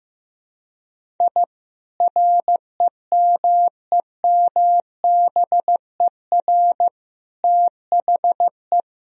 Morse code, heard as a keyed tone, about 16 wpm, 700 Hz. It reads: I REMEMBER THE
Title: Sentences from Top 1000 Words - Review Part 1 - 15wpm